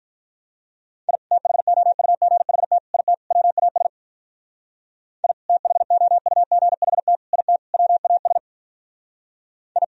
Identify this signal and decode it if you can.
Morse code, heard as a keyed tone, about 42 wpm, 700 Hz. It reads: I THOUGHT IT WAS I THOUGHT IT WAS I
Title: Speed-Racing - Sets of 4 Words 35wpm